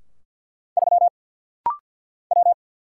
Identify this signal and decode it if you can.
Morse code, heard as a keyed tone, about 50 wpm, 690 Hz. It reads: 3 W
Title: Single Letter-Number Alt - Rapid-Fire 50wpm